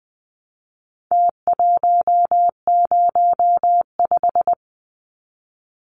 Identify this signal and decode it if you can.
Morse code, heard as a keyed tone, about 20 wpm, 700 Hz. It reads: T105
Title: Numbers - Contest Style - 4-Digit 20wpm